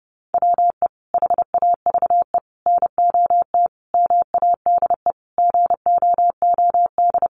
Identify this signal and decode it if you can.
Morse code, heard as a keyed tone, about 30 wpm, 700 Hz. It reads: WE HAVE NOT MADE GOOD
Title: Sentences from Top 1400 Words 30wpm